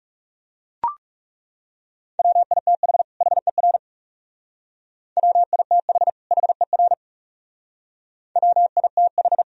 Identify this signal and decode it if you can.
Morse code, heard as a keyed tone, about 35 wpm, 700 Hz. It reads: WITH HER WITH HER WITH
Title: Speed-Racing - Sets of 2 Words 30wpm